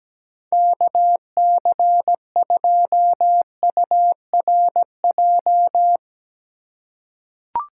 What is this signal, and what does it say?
Morse code, heard as a keyed tone, about 17 wpm, 700 Hz. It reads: KC2URJ
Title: USA Call Signs - 6 Characters - 2x Character Spacing - 17wpm